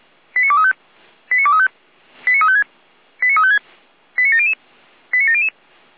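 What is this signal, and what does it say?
Five-tone selective call (ZVEI), heard as quick runs of five tones, tones of about 70 ms. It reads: 89235 89235 89356 89356 89700 89700
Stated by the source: Five-tone ZVEI selective call received by SDR, from feuerwehr-orlamuende.org audio archive (fire brigade Orlamünde).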